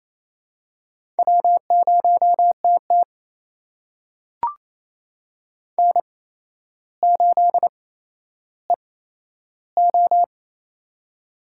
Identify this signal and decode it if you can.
Morse code, heard as a keyed tone, about 28 wpm, spaced as for 12 wpm, 700 Hz. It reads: W0TT N8EO
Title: USA Call Signs - 4 Characters - 8x Character Spacing - 28wpm